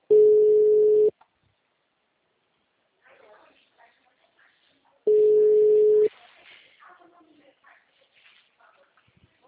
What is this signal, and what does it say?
Ringback tone (Europe, 425 Hz), with speech.